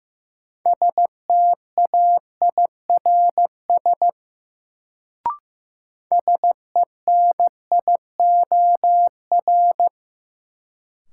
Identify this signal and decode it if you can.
Morse code, heard as a keyed tone, about 15 wpm, 700 Hz. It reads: STAIRS SENIOR
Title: Sets of 5 Words Encore - New Word Drill - 15wpm